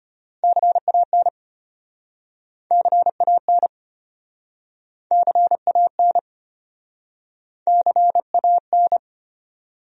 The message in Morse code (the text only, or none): CAN CAN CAN CAN